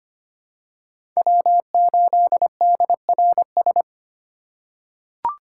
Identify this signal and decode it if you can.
Morse code, heard as a keyed tone, about 25 wpm, 700 Hz. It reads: W8DRS
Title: USA Call Signs - 5 Characters - 3x Character Spacing - 25wpm